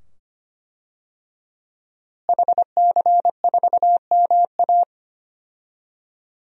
Morse code, heard as a keyed tone, about 25 wpm, 700 Hz. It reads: HC4MA
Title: DXCC-100 Call Signs - 5 Characters - Repeated 2x - 25wpm